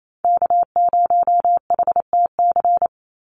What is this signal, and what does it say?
Morse code, heard as a keyed tone, about 28 wpm, 700 Hz. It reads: K0HTC
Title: USA Call Signs 28wpm